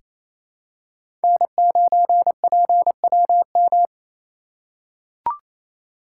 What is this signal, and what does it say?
Morse code, heard as a keyed tone, about 28 wpm, 700 Hz. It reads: N9PWM
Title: USA Call Signs - 5 Characters - 4x Character Spacing - 28wpm